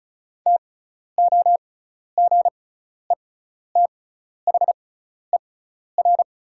Morse code, heard as a keyed tone, about 36 wpm, spaced as for 12 wpm, 700 Hz. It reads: TOGETHER
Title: 6x Character Spacing - Top 300 Words - 35wpm